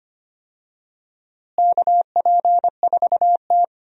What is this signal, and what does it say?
Morse code, heard as a keyed tone, about 25 wpm, 700 Hz. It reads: KP4T
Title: DXCC-100 Call Signs - 4 Characters - Repeated 3x - 25wpm